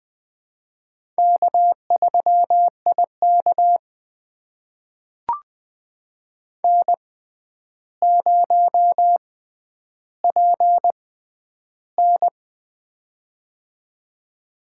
Morse code, heard as a keyed tone, about 20 wpm, spaced as for 10 wpm, 700 Hz. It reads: K3IK N0PN
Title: USA Call Signs - 4 Characters - 6x Character Spacing - 20wpm